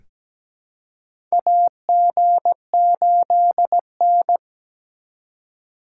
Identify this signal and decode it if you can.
Morse code, heard as a keyed tone, about 17 wpm, 700 Hz. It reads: AG8N
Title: USA Call Signs - 4 Characters - 4x Character Spacing - 17wpm